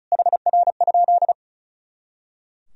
Morse code, heard as a keyed tone, about 36 wpm, 700 Hz. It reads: HR?
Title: Sentences from Top 300 Words and Abbreviations 35wpm